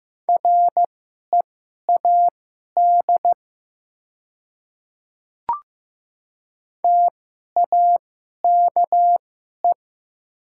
Morse code, heard as a keyed tone, about 15 wpm, spaced as for 11 wpm, 700 Hz. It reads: READ TAKE
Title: 2x Character Spacing - Top 200 Words - 15wpm